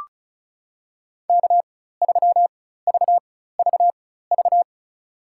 Morse code, heard as a keyed tone, about 36 wpm, spaced as for 16 wpm, 700 Hz. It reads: K3VVV
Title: USA Call Signs - 5 Characters - 4x Character Spacing - 35wpm